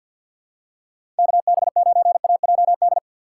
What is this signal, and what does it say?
Morse code, heard as a keyed tone, about 49 wpm, 700 Hz. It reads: KB9AJD
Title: USA Call Signs - 4x Character Spacing - 50wpm